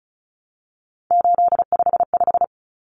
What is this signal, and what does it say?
Morse code, heard as a keyed tone, about 34 wpm, 700 Hz. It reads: 855
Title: Numbers - 3-Digit 35wpm